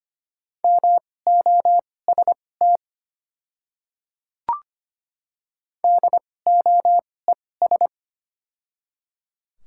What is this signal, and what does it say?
Morse code, heard as a keyed tone, about 25 wpm, spaced as for 18 wpm, 700 Hz. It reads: MOST DOES
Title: 2x Character Spacing - Top 200 Words - 25wpm